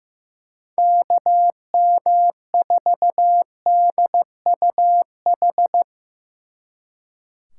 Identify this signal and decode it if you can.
Morse code, heard as a keyed tone, about 16 wpm, 700 Hz. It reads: KM4DUH
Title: Speed Racing - USA 2x3 Call Signs 15wpm